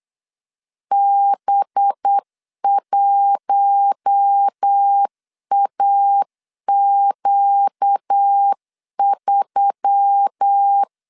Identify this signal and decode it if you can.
Morse code, heard as a keyed tone, about 8 wpm, 800 Hz. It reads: B1AQ3
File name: LEZIONI / 7 GF9Z / 7 G 8wpm 3118